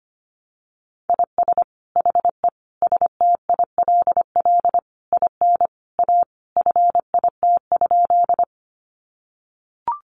Morse code, heard as a keyed tone, about 25 wpm, 700 Hz. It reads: IS HE STILL IN A FIT?
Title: Sentences from Top 800 Words 25wpm